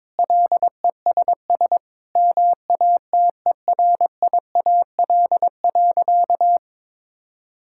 Morse code, heard as a keyed tone, about 22 wpm, 700 Hz. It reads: LESS MATERIAL.
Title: Sentences from Top 1000 Words - Review Part 5 - 22wpm